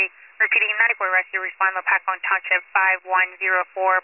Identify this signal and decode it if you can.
Emergency Alert System (SAME) tones, heard as harsh buzzing bursts, with speech. It is heard from 1.0 to 3.6 s.